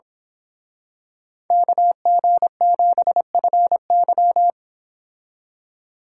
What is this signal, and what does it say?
Morse code, heard as a keyed tone, about 25 wpm, 700 Hz. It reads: KG7FY
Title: Speed Racing - USA 2x2 Call Signs 22wpm